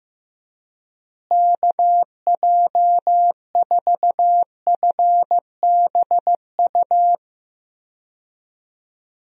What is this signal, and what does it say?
Morse code, heard as a keyed tone, about 15 wpm, 700 Hz. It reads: KJ4FBU
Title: USA Call Signs - 6 Characters - Repeated 2x - 15wpm